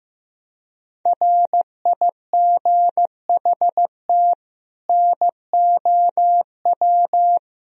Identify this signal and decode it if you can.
Morse code, heard as a keyed tone, about 16 wpm, 700 Hz. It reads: RIGHT NOW
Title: Speed-Racing - Sets of 2 Words 15wpm